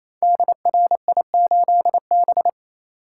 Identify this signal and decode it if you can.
Morse code, heard as a keyed tone, about 27 wpm, 700 Hz. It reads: DRI8B
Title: Code Groups - 5 Letter:Numbers - 28wpm